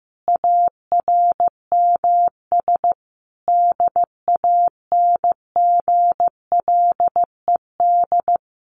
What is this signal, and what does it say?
Morse code, heard as a keyed tone, about 14 wpm, 700 Hz. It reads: ARMS DANGLED
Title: A Sound Of Thunder 15wpm